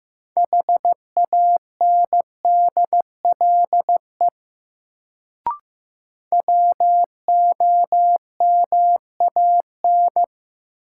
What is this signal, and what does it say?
Morse code, heard as a keyed tone, about 15 wpm, 700 Hz. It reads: HANDLE WOMAN
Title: Sentences from Top 1100 Words - New Word Drill Alt - 15wpm